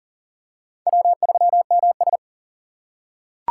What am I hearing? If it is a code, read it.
Morse code, heard as a keyed tone, about 40 wpm, 700 Hz. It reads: W3MS
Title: K1USN SST QSO Practice - 40wpm